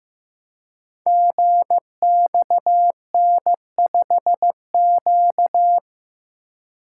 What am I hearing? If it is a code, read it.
Morse code, heard as a keyed tone, about 15 wpm, 700 Hz. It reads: GXN5Q
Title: Code Groups - 5 Letter:Numbers - 15wpm